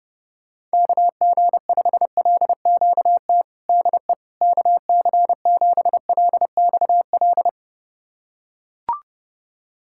Morse code, heard as a keyed tone, about 30 wpm, 700 Hz. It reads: KG5LQT DE KC7LXL
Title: USA Callsign DE Callsign - 6 Characters - 30wpm